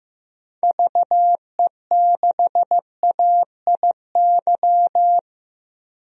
Morse code, heard as a keyed tone, about 15 wpm, 700 Hz. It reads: VE6AIY
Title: DXCC-100 Call Signs - 6 Characters - Repeated 3x - 15wpm